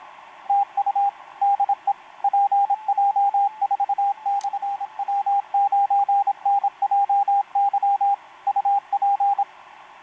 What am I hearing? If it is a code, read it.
Morse code, heard as a keyed tone, about 26 wpm, 800 Hz, with radio noise. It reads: TU DE PJ4/W9NJY UP
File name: pj4-w9njy f5upo etc 12m cw 28mar2011 1924